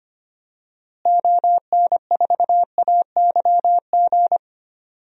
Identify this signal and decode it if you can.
Morse code, heard as a keyed tone, about 25 wpm, 700 Hz. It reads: ON4AYG
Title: DXCC-100 Call Signs - 6 Characters - Repeated 2x - 25wpm